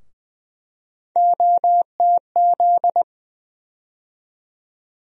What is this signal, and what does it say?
Morse code, heard as a keyed tone, about 20 wpm, 700 Hz. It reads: OTZ
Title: USA Call Sign Postfixes - 3 Letters 20wpm